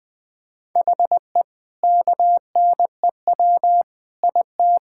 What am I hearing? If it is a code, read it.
Morse code, heard as a keyed tone, about 20 wpm, 700 Hz. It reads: HE KNEW IT
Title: Sentences from Top 1000 Words - Review Part 6 - 20wpm